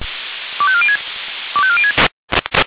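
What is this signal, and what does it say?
Five-tone selective call (ZVEI), heard as quick runs of five tones, tones of about 70 ms. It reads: 25406 25406
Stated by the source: ZVEI five-tone call 25304 received by SDR.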